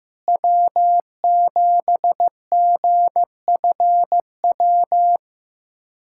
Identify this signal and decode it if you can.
Morse code, heard as a keyed tone, about 15 wpm, 700 Hz. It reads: W7GFW
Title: K1USN SST QSO Practice - 15wpm